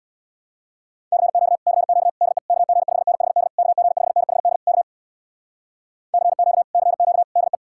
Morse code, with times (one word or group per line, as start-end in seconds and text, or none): 1.1-1.6 CQ
1.7-2.1 CQ
2.2-2.4 DE
2.5-3.5 KG5NHD
3.6-4.6 KG5NHD
4.7-4.8 K
6.1-6.6 CQ
6.7-7.2 CQ
7.4-7.6 DE